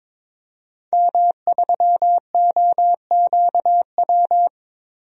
Morse code, heard as a keyed tone, about 22 wpm, 700 Hz. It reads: M3OQW
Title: DXCC-100 Call Signs - 5 Characters - 22wpm